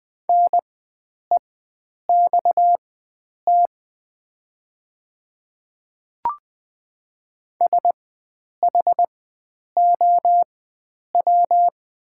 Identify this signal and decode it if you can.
Morse code, heard as a keyed tone, about 20 wpm, spaced as for 9 wpm, 700 Hz. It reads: NEXT SHOW
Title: 4x Character Spacing - Top 300 Words - 20wpm